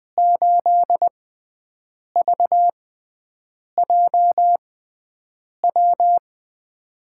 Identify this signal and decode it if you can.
Morse code, heard as a keyed tone, about 20 wpm, spaced as for 7 wpm, 700 Hz. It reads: 8VJW
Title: USA Call Signs - 6 Characters - 6x Character Spacing - 20wpm